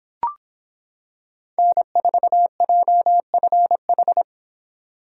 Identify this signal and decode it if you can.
Morse code, heard as a keyed tone, about 26 wpm, 700 Hz. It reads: N4JFH